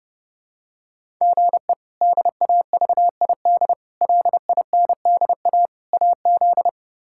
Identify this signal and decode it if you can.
Morse code, heard as a keyed tone, about 32 wpm, 700 Hz. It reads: GE DAVID LINDA AZ
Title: K1USN SST QSO Practice - 30wpm